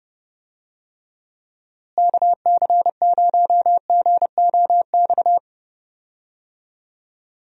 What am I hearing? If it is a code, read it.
Morse code, heard as a keyed tone, about 31 wpm, 700 Hz. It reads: KC0GOX
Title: USA Call Signs - 6 Characters - Repeated 3x - 30wpm